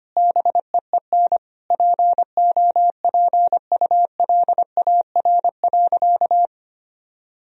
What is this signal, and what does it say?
Morse code, heard as a keyed tone, about 25 wpm, 700 Hz. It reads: BEEN POPULAR.
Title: Sentences from Top 1000 Words - Review Part 3 - 25wpm